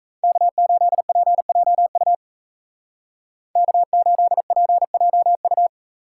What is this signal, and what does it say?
Morse code, heard as a keyed tone, about 33 wpm, 700 Hz. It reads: K8PJU K8PJU
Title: Speed Racing - USA 1x3 Call Signs 28wpm